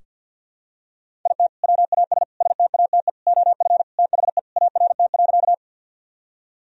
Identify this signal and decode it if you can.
Morse code, heard as a keyed tone, about 47 wpm, 700 Hz. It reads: IT WAS STATE OF THE ART.
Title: Sentences from Top 1000 Words - Review Part 3 - 50wpm